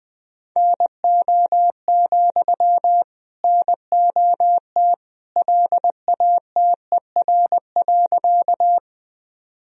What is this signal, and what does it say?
Morse code, heard as a keyed tone, about 20 wpm, 700 Hz. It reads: NO, NOT LATER.
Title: Sentences from Top 1000 Words - Review Part 4 - 20wpm